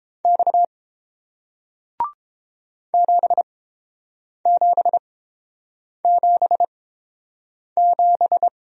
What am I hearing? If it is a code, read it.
Morse code, heard as a keyed tone, about 30 wpm, spaced as for 7 wpm, 700 Hz. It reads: X 7777